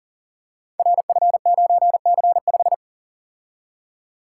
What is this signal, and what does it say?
Morse code, heard as a keyed tone, about 41 wpm, 690 Hz. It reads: RF9C5
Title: Code Groups - 5 Letter:Numbers - 40wpm